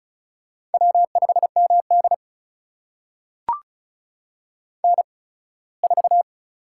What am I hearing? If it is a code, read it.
Morse code, heard as a keyed tone, about 36 wpm, spaced as for 21 wpm, 700 Hz. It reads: W5MD N4
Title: USA Call Signs - 8x Character Spacing - 35wpm